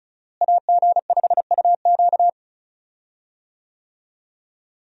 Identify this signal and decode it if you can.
Morse code, heard as a keyed tone, about 36 wpm, 700 Hz. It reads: AG5UQ
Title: USA Call Signs - 5 Characters - Repeated 3x - 35wpm